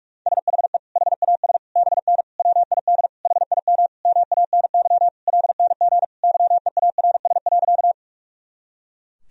Morse code, heard as a keyed tone, about 42 wpm, 700 Hz. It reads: SHE HAS BN WID HIM MANY LNG YEARS.
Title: Sentences from Top 200 Words and Abbreviations 45wpm